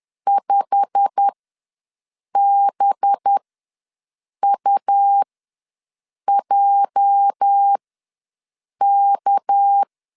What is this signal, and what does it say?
Morse code, heard as a keyed tone, about 10 wpm, spaced as for 6 wpm, 800 Hz. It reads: 5BUJK
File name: LEZIONI / 8 YJ45 / 8 YJ45 ws 10wpm 3843